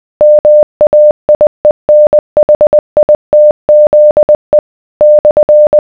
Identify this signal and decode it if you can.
Morse code, heard as a keyed tone, about 20 wpm, 600 Hz. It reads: MAIENHITZE /